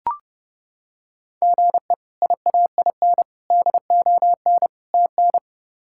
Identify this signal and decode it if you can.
Morse code, heard as a keyed tone, about 32 wpm, 700 Hz. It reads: GE IAIN DON TN
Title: K1USN SST QSO Practice - 30wpm